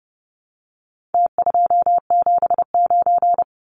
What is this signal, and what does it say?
Morse code, heard as a keyed tone, about 30 wpm, 700 Hz. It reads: T279
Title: Numbers - Contest Style - 4-Digit 30wpm